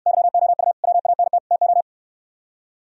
Morse code, heard as a keyed tone, about 67 wpm, 700 Hz. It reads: YOU WANT TO